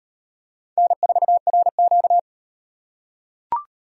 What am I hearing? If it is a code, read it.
Morse code, heard as a keyed tone, about 34 wpm, 700 Hz. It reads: N4RQ